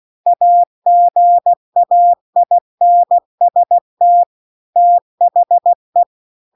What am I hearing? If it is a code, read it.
Morse code, heard as a keyed tone, about 16 wpm, 700 Hz. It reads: AGAINST THE